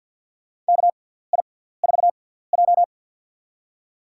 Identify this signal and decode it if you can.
Morse code, heard as a keyed tone, about 48 wpm, spaced as for 17 wpm, 700 Hz. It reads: KI4J
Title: USA Call Signs - 4 Characters - 6x Character Spacing - 50wpm